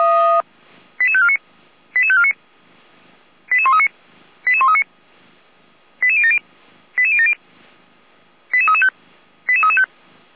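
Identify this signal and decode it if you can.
Five-tone selective call (ZVEI), heard as quick runs of five tones, tones of about 80 ms. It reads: 88549 88549 88139 88139 88070 88070 88335 88335
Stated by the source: ZVEI five-tone selective call 88549 received on SDR, Feuerwehr Orlamünde.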